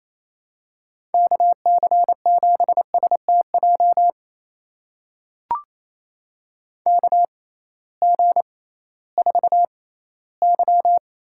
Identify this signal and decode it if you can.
Morse code, heard as a keyed tone, about 28 wpm, spaced as for 16 wpm, 700 Hz. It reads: KC7STJ KG4Y
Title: USA Call Signs - 6 Characters - 6x Character Spacing - 28wpm